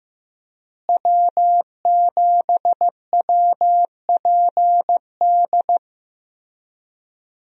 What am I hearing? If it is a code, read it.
Morse code, heard as a keyed tone, about 15 wpm, 700 Hz. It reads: W7WPD
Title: USA Call Signs - 5 Characters - Repeated 2x - 15wpm